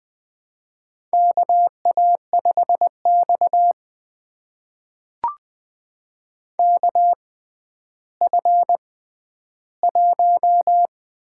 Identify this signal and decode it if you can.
Morse code, heard as a keyed tone, about 20 wpm, spaced as for 11 wpm, 700 Hz. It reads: KA5X KF1
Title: USA Call Signs - 4 Characters - 6x Character Spacing - 20wpm